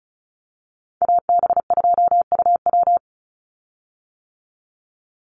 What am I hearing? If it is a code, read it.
Morse code, heard as a keyed tone, about 34 wpm, 700 Hz. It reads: AB2UW
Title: USA Call Signs 35wpm